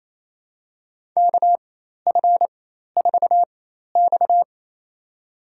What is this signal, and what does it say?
Morse code, heard as a keyed tone, about 28 wpm, spaced as for 13 wpm, 700 Hz. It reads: KF4X
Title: USA Call Signs - 4 Characters - 4x Character Spacing - 28wpm